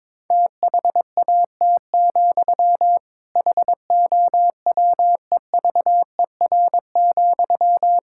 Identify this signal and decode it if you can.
Morse code, heard as a keyed tone, about 21 wpm, 700 Hz. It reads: THAT, HOWEVER,
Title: Sentences from Top 1000 Words - Review Part 6 - 22wpm